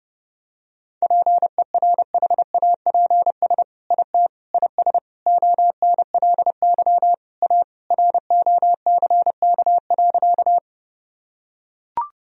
Morse code, heard as a keyed tone, about 32 wpm, 700 Hz. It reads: PERHAPS IT IS ONLY A ROCK.
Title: Sentences from Top 1000 Words - Review Part 2 - 30wpm